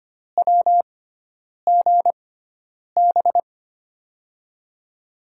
Morse code, heard as a keyed tone, about 25 wpm, spaced as for 9 wpm, 700 Hz. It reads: WGB
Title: USA Call Signs - 6 Characters - 6x Character Spacing - 25wpm